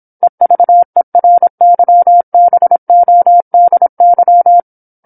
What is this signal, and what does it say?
Morse code, heard as a keyed tone, about 27 wpm, 700 Hz. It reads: EVERYBODY
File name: 08-Chapter-08 26 WPM